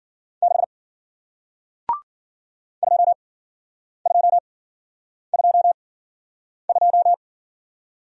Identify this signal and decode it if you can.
Morse code, heard as a keyed tone, about 48 wpm, spaced as for 9 wpm, 700 Hz. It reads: 6 2222
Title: Speed-Racing Single Letter-Number 40wpm